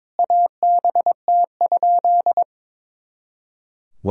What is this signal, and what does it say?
Morse code, heard as a keyed tone, about 21 wpm, 700 Hz. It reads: ABT?
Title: Sentences from Top 100 Words and Abbreviations 22wpm